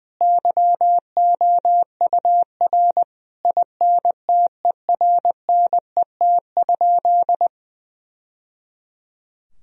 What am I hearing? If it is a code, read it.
Morse code, heard as a keyed tone, about 20 wpm, 700 Hz. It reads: YOUR INTERNET?
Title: Sentences from Top 1000 Words - Review Part 4 - 20wpm